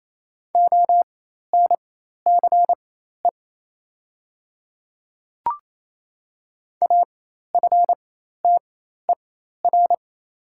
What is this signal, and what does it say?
Morse code, heard as a keyed tone, about 28 wpm, spaced as for 13 wpm, 700 Hz. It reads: ONCE AFTER
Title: 4x Character Spacing - Top 500 Words - 28wpm